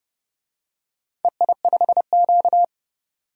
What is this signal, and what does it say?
Morse code, heard as a keyed tone, about 30 wpm, 700 Hz. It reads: EI5Q
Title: DXCC-100 Call Signs - 4 Characters - 30wpm